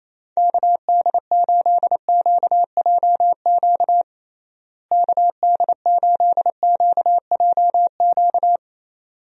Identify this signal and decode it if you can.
Morse code, heard as a keyed tone, about 28 wpm, 700 Hz. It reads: KD8QJQ KD8QJQ
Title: USA Call Signs - 6 Characters - Repeated 2x - 28wpm